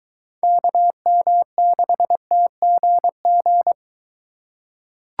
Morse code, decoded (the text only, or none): KM6TGG